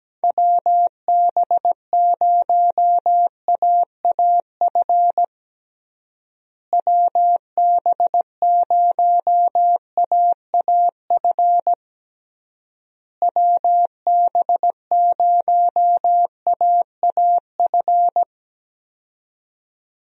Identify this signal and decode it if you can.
Morse code, heard as a keyed tone, about 17 wpm, 700 Hz. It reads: WB0AAF WB0AAF WB0AAF
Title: USA Call Signs - 6 Characters - Repeated 3x - 17wpm